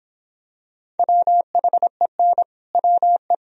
Morse code, heard as a keyed tone, about 24 wpm, 700 Hz. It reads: WHEN WE